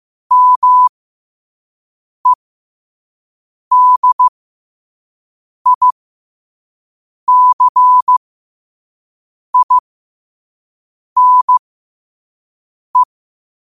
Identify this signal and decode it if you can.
Morse code, heard as a keyed tone, about 15 wpm, spaced as for 5 wpm, 1000 Hz. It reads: MEDICINE
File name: qotdfiles / 5wpm